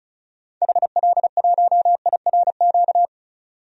Morse code, heard as a keyed tone, about 34 wpm, 700 Hz. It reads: HL1IRQ
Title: DXCC-100 Call Signs - 6 Characters - 35wpm